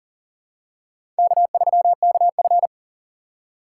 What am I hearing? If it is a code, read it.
Morse code, heard as a keyed tone, about 39 wpm, 700 Hz. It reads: K3KF
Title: USA Call Signs - 4 Characters - 3x Character Spacing - 40wpm